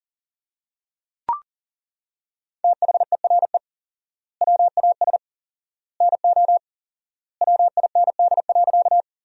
Morse code, heard as a keyed tone, about 37 wpm, 700 Hz. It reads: THERE WAS NO WIND.
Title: 4x Word-Spacing - Sentences from Top 400 Words - 40wpm